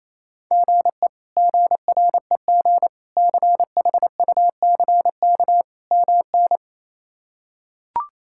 Morse code, heard as a keyed tone, about 27 wpm, 700 Hz. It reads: GE GREG CHUCK MN
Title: K1USN SST QSO Practice - 28wpm